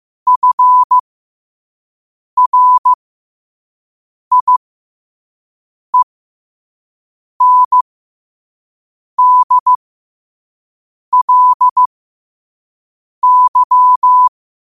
Morse code, heard as a keyed tone, about 15 wpm, spaced as for 5 wpm, 1000 Hz. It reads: FRIENDLY
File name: qotdfiles / 5wpm